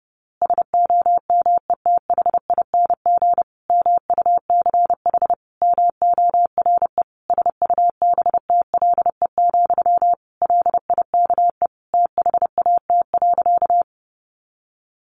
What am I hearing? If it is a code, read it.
Morse code, heard as a keyed tone, about 30 wpm, 700 Hz. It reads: SOMETHING MUCH MORE SUBTLE, LIKE THAT.
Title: A Sound Of Thunder 30wpm